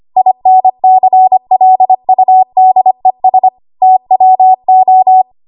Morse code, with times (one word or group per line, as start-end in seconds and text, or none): 0.2-3.5 INCLUDES
3.8-5.2 TWO